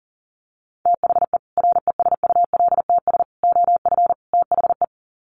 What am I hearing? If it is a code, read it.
Morse code, heard as a keyed tone, about 40 wpm, 700 Hz. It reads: THE RESULTS OF THE
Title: Sets of 4 Words 40wpm